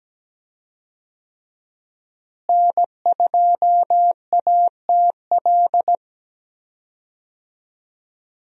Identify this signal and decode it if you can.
Morse code, heard as a keyed tone, about 17 wpm, 700 Hz. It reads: N2ATL
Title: USA Call Signs - 5 Characters - Repeated 3x - 17wpm